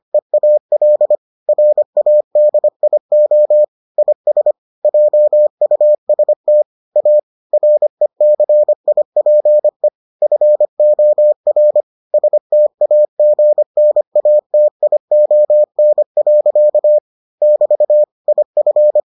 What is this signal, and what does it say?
Morse code, heard as a keyed tone, about 25 wpm, 600 Hz. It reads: EAL RADIO IS JUST A RECIPE FOR STAGNATION. <BT> IF